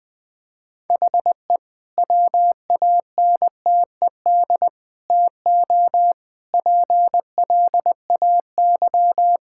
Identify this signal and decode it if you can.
Morse code, heard as a keyed tone, about 20 wpm, 700 Hz. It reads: HE WANTED TO PLAY
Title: Sentences with 3-Word Phrases - 20wpm